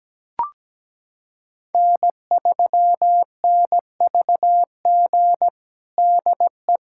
Morse code, heard as a keyed tone, about 17 wpm, 700 Hz. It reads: N3NVG DE
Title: USA Callsign DE Callsign - 5 Characters - 17wpm